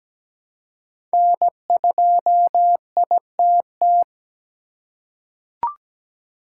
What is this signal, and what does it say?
Morse code, heard as a keyed tone, about 17 wpm, 700 Hz. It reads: N2ITT
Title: USA Call Signs - 5 Characters - 8x Character Spacing - 17wpm